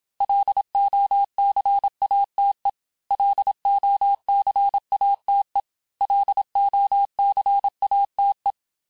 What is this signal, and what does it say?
Morse code, heard as a keyed tone, about 27 wpm, 800 Hz. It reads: LOCATE LOCATE LOCATE
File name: cours-du-mercredi / audios / cours-869-texte-LES-MOTS-vitesse-25-MPM